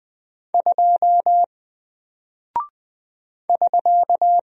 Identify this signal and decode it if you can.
Morse code, heard as a keyed tone, about 20 wpm, 700 Hz. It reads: 2 <SK>
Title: Single Character (Letter:Number:Punctuation:Prosign) Alt 20wpm